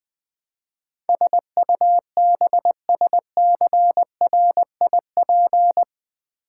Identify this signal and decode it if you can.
Morse code, heard as a keyed tone, about 20 wpm, 700 Hz. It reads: SUBSCRIP
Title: Build A Word - 4 Letters At A Time - 12-Letter Words - 20wpm